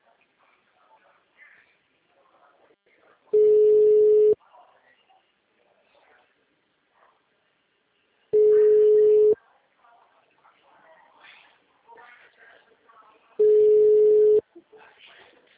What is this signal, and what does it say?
Ringback tone (Europe, 425 Hz), with speech.
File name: out- -2004- - - .68596